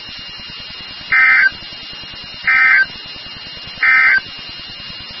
Emergency Alert System (SAME) tones, heard as harsh buzzing bursts. It is heard from 1.0 to 4.3 s.
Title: NOAA Weather Radio WXK36